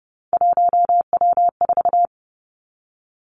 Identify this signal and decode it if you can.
Morse code, heard as a keyed tone, about 30 wpm, 700 Hz. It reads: 1W4
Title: Three Challenging Letter-Number Combinations 30wpm